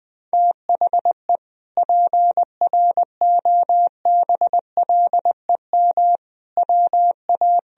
Morse code, heard as a keyed tone, about 21 wpm, 700 Hz. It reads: THE PROBLEM WA
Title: Sentences from Top 1000 Words - Review Part 2 - 20wpm